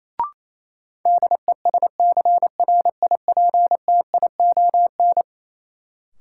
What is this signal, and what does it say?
Morse code, heard as a keyed tone, about 28 wpm, 700 Hz. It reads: DESCRIPTION
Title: Sentences from Top 1300 Words - New Word Drill - 28wpm